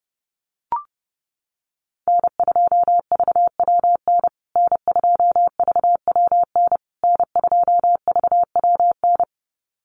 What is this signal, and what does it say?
Morse code, heard as a keyed tone, about 30 wpm, 700 Hz. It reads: N2VWN N2VWN N2VWN
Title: USA Call Signs Repeated 3x 30wpm